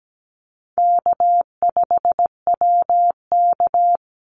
Morse code, heard as a keyed tone, about 17 wpm, 700 Hz. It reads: K5WK
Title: Easy Call Signs Repeated 3x - 17wpm